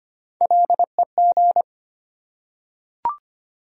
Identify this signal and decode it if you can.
Morse code, heard as a keyed tone, about 25 wpm, 700 Hz. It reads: LEG
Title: Three Letter Words - 25wpm